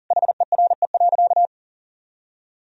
Morse code, heard as a keyed tone, about 38 wpm, 700 Hz. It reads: HERE.
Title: Sentences from Top 1000 Words - Review Part 3 - 40wpm